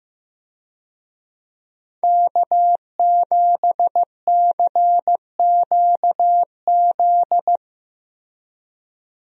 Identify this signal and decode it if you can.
Morse code, heard as a keyed tone, about 15 wpm, 700 Hz. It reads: K7CQZ
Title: USA Call Signs - 5 Characters - Repeated 3x - 15wpm